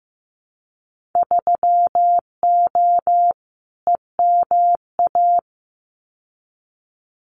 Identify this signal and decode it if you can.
Morse code, heard as a keyed tone, about 15 wpm, 700 Hz. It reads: 3O EMA
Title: Winter Field Day Exchanges 15wpm